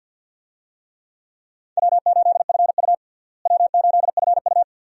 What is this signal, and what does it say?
Morse code, heard as a keyed tone, about 49 wpm, 700 Hz. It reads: W8FU W8FU
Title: USA Call Signs - 4 Characters - Repeated 2x - 50wpm